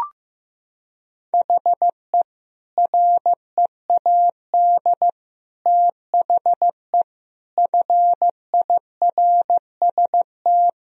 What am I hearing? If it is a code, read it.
Morse code, heard as a keyed tone, about 15 wpm, 700 Hz. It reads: HE READ THE FIRST
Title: Sentences from Top 1000 Words - Review Part 1 - 15wpm